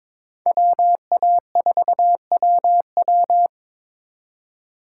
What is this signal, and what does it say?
Morse code, heard as a keyed tone, about 26 wpm, 700 Hz. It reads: WA4WW
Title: Speed Racing - USA 2x2 Call Signs 22wpm